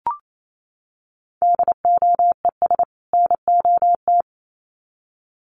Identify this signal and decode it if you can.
Morse code, heard as a keyed tone, about 28 wpm, 700 Hz. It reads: DOES NOT
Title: Sets of 2 Words 28wpm